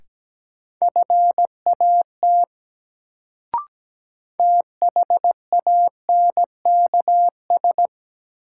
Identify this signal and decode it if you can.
Morse code, heard as a keyed tone, about 17 wpm, 700 Hz. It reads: FAT THANKS
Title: Sentences from Top 600 Words - New Word Drill - 17wpm